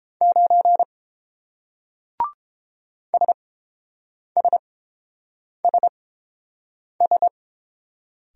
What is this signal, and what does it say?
Morse code, heard as a keyed tone, about 29 wpm, spaced as for 7 wpm, 700 Hz. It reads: 9 SSSS